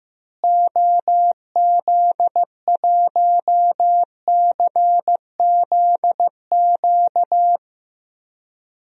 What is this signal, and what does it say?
Morse code, heard as a keyed tone, about 15 wpm, 700 Hz. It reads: OZ1CZQ
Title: DXCC-100 Call Signs - 6 Characters - Repeated 2x - 15wpm